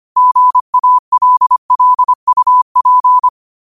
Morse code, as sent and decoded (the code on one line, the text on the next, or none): --. .- .-.. .-.. ..- .--.
GALLUP